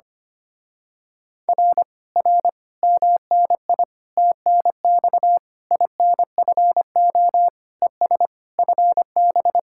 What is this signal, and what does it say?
Morse code, heard as a keyed tone, about 24 wpm, 700 Hz. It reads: R R MNI TNX INFO ES FB
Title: Speed-Racing Common QSO Phrases - 25wpm